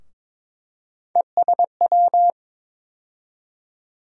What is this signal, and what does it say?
Morse code, heard as a keyed tone, about 22 wpm, 700 Hz. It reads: ESW
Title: USA Call Sign Postfixes - 3 Letters 22wpm